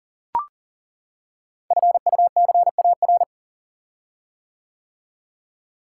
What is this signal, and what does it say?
Morse code, heard as a keyed tone, about 40 wpm, 690 Hz. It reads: FUCAR
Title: Code Groups - 5 Letter:Numbers - 40wpm